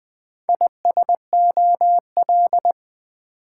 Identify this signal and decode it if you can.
Morse code, heard as a keyed tone, about 20 wpm, 700 Hz. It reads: ISOL
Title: Build A Word - 4 Letters At A Time - 8-Letter Words - 20wpm